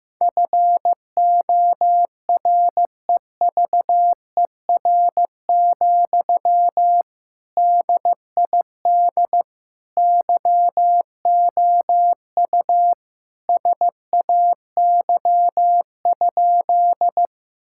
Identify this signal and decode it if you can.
Morse code, heard as a keyed tone, about 15 wpm, 700 Hz. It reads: FOREVER, DID YOU SAY?
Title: Sentences from Top 1600 Words 15wpm